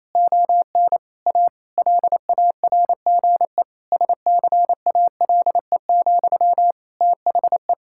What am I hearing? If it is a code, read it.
Morse code, heard as a keyed tone, about 28 wpm, 700 Hz. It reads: ON A LARGE SCALE, THE
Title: Sentences from Top 1000 Words - Review Part 5 - 28wpm